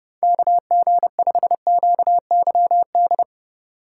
Morse code, heard as a keyed tone, about 30 wpm, 700 Hz. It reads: KG5QYD
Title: USA Call Signs - 6 Characters - 2x Character Spacing - 30wpm